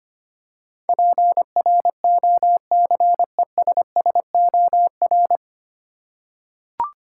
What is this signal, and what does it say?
Morse code, heard as a keyed tone, about 25 wpm, 700 Hz. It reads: PROCESSOR
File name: Nine Letter Words - 25wpm